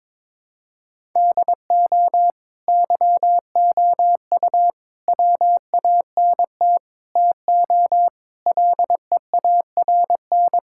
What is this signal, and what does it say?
Morse code, heard as a keyed tone, about 22 wpm, 700 Hz. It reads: DO YOU WANT TO LEARN
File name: Sentences with 7-Word Phrases - 22wpm